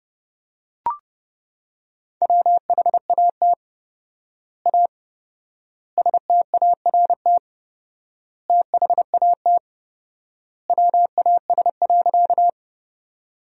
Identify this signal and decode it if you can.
Morse code, heard as a keyed tone, about 30 wpm, 700 Hz. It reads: WHAT A START THAT WAS.
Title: 4x Word-Spacing - Sentences from Top 300 Words - 30wpm